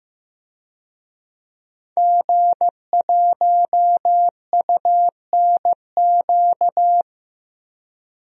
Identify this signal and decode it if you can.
Morse code, heard as a keyed tone, about 15 wpm, 700 Hz. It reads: G1UNQ
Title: DXCC-100 Call Signs - 5 Characters - Repeated 3x - 15wpm